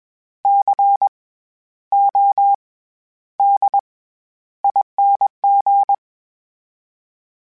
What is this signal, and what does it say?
Morse code, heard as a keyed tone, about 21 wpm, 800 Hz. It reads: C O D ING
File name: Special INGs-mp3s / Special INGs-mp3s / SlowToFaster ing @20